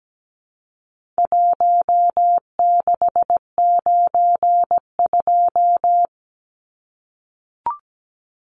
Morse code, heard as a keyed tone, about 17 wpm, 700 Hz. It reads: 1692
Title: Numbers - 4-Digit 17wpm